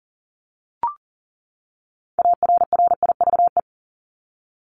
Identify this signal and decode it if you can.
Morse code, heard as a keyed tone, about 40 wpm, 700 Hz. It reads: ARRIVE
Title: ED Suffix - Top 100 Words with Base Words 40wpm